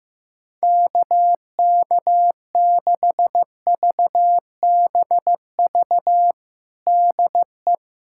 Morse code, heard as a keyed tone, about 15 wpm, 700 Hz. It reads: KK6VBV DE
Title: USA Callsign DE Callsign - 6 Characters - 15wpm